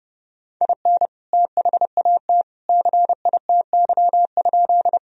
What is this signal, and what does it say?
Morse code, heard as a keyed tone, about 30 wpm, 700 Hz. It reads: IN THAT CITY?
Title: Sentences from Top 1000 Words - Review Part 1 - 30wpm